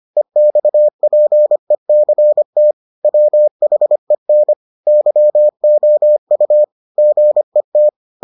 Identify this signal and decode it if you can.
Morse code, heard as a keyed tone, about 25 wpm, 600 Hz. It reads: EXPECT WHEN YOU GET